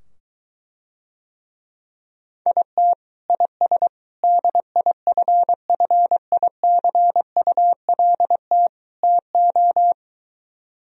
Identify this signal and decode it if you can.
Morse code, heard as a keyed tone, about 24 wpm, 700 Hz. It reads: IT IS DIFFICULT TO